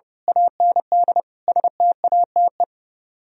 Morse code, heard as a keyed tone, about 24 wpm, 700 Hz. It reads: AND STATE
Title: Speed-Racing - Sets of 5 Words 25wpm